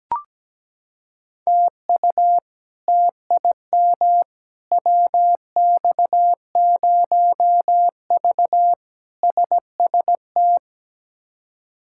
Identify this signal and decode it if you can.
Morse code, heard as a keyed tone, about 17 wpm, 700 Hz. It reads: TU TIM WX0V SST
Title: K1USN SST QSO Practice - 17wpm